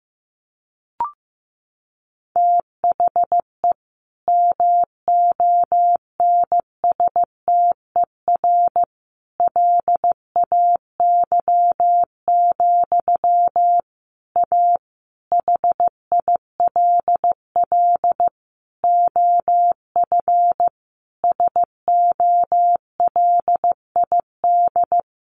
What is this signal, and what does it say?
Morse code, heard as a keyed tone, about 15 wpm, 700 Hz. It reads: THE MONSTER LAY, A HILL OF SOLID
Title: A Sound Of Thunder 15wpm